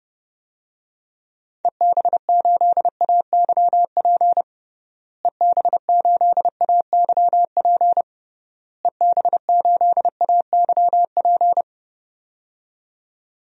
Morse code, heard as a keyed tone, about 30 wpm, 700 Hz. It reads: EB8AYP EB8AYP EB8AYP
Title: DXCC-100 Call Signs - 6 Characters - Repeated 3x - 30wpm